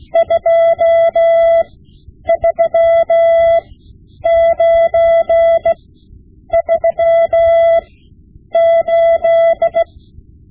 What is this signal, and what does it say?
Morse code, heard as a keyed tone, about 16 wpm, spaced as for 9 wpm, 670 Hz. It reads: 23938
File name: Morse Lessons / Morse Lessons / Tape-5B